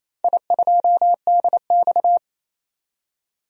Morse code, heard as a keyed tone, about 28 wpm, 700 Hz. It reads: I2DX
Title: DXCC-100 Call Signs - 4 Characters - 28wpm